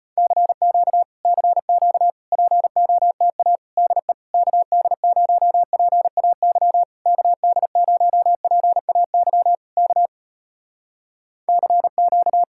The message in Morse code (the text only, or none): CQ CQ POTA DE KD0PAY KD0PAY K CQ